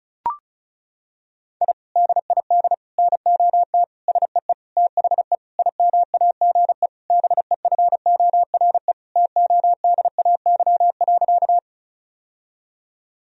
Morse code, heard as a keyed tone, about 36 wpm, 700 Hz. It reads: I DID NOT SEE THE IMAGE BEFORE TODAY.
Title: Sentences from Top 1000 Words - Review Part 4 - 35wpm